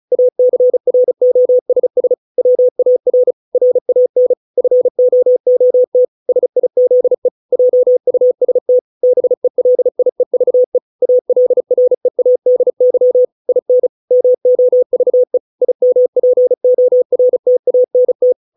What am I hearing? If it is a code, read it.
Morse code, heard as a keyed tone, about 34 wpm, 500 Hz. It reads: ACROSS WAR RAN FOOT SIZE JUST BELIEVE ALREADY IN MOVE IMPORTANT
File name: 632 words / 632 Words 35WPM